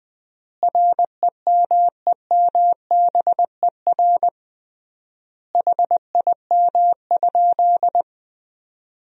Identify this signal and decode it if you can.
Morse code, heard as a keyed tone, about 21 wpm, 700 Hz. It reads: REMEMBER HIM?
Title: 3x Word-Spacing - Sentences from Top 400 Words - 20wpm